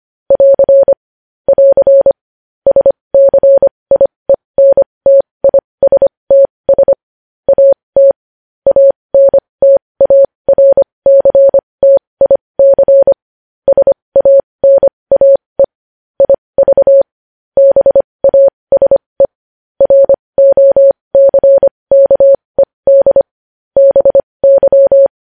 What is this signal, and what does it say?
Morse code, heard as a keyed tone, about 25 wpm, spaced as for 20 wpm, 550 Hz. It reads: <AR> <AR> SCIENTISTS AT ANTARCTIC SANAE IV BASE ROCKED BY
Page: news-headlines-in-morse-code-at-20-wpm